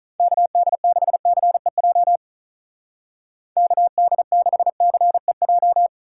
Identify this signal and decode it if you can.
Morse code, heard as a keyed tone, about 43 wpm, 700 Hz. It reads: KD6CEJ KD6CEJ
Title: Speed Racing - USA 2x3 Call Signs 35wpm